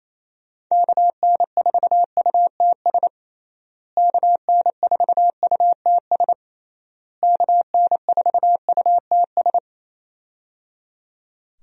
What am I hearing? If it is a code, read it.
Morse code, heard as a keyed tone, about 27 wpm, 700 Hz. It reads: KN4UTS KN4UTS KN4UTS
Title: USA Call Signs - 6 Characters - Repeated 3x - 28wpm